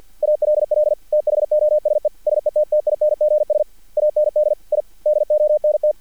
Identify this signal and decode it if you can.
Morse code, heard as a keyed tone, about 50 wpm, 600 Hz, with radio noise. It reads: ALL THOSE SETTINGS AND I DONT
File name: archived / CRSnet